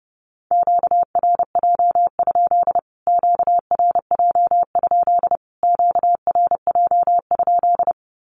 Morse code, heard as a keyed tone, about 30 wpm, 700 Hz. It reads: QRJ? QRJ? QRJ?
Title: Q-Codes Repeated 3x 30wpm